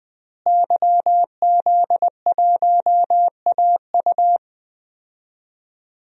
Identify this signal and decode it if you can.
Morse code, heard as a keyed tone, about 20 wpm, 700 Hz. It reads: YZ1AU
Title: DXCC-100 Call Signs - 5 Characters - Repeated 2x - 20wpm